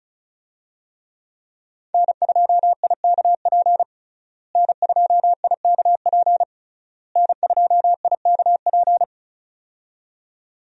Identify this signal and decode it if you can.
Morse code, heard as a keyed tone, about 34 wpm, 700 Hz. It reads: N2IKP N2IKP N2IKP
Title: USA Call Signs - 5 Characters - Repeated 3x - 35wpm